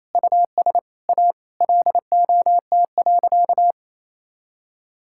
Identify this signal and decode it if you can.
Morse code, heard as a keyed tone, about 27 wpm, 700 Hz. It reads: US A LOT.
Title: Sentences from Top 1000 Words - Review Part 5 - 28wpm